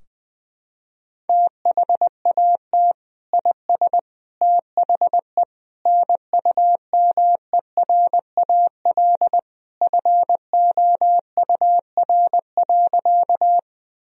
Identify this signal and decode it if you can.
Morse code, heard as a keyed tone, about 21 wpm, 700 Hz. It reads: THAT IS THE NUMERAL FOUR.
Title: Sentences from Top 1000 Words - Review Part 2 - 20wpm